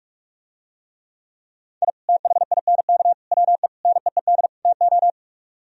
Morse code, heard as a keyed tone, about 44 wpm, 700 Hz. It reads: I THINK WE NEED TO